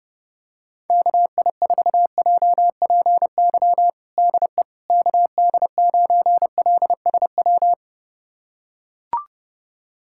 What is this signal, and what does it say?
Morse code, heard as a keyed tone, about 30 wpm, 700 Hz. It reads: KI4JPY DE KD9LSW
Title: USA Callsign DE Callsign 30wpm